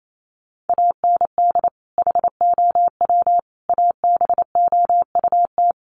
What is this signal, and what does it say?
Morse code, heard as a keyed tone, about 28 wpm, 700 Hz. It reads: AND HOW ABOUT
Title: Sentences from Top 1200 Words 28wpm